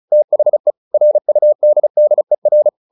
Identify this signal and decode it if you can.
Morse code, heard as a keyed tone, about 34 wpm, 600 Hz. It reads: THE RUDDER
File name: the old man and the sea 2 35wpm